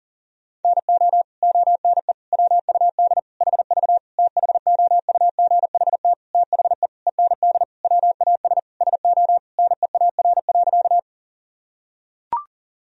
Morse code, heard as a keyed tone, about 40 wpm, 700 Hz. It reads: NO ONE WUD HV THOUGHT THE END WAS SO NEAR.
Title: Sentences from Top 300 Words and Abbreviations 40wpm